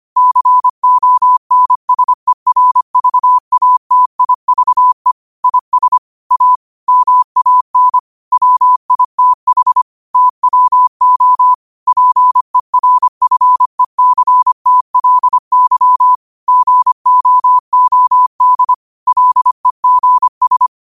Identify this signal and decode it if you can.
Morse code, heard as a keyed tone, about 25 wpm, 1000 Hz. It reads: CONSERVATIVE IS A MAN WITH TWO PERFECTLY GOOD LEGS
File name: qotdfiles / 25wpm